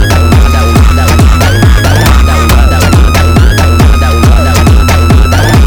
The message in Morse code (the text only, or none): none